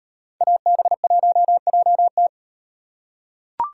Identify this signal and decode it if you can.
Morse code, heard as a keyed tone, about 37 wpm, 700 Hz. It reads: AB1JT